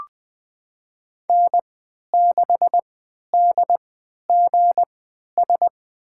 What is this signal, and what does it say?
Morse code, heard as a keyed tone, about 20 wpm, spaced as for 11 wpm, 700 Hz. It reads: N6DGS
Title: USA Call Signs - 5 Characters - 3x Character Spacing - 20wpm